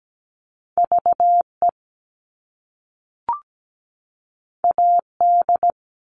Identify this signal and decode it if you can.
Morse code, heard as a keyed tone, about 17 wpm, 700 Hz. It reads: VE AD
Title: All Cut Numbers - 2-Digit 17wpm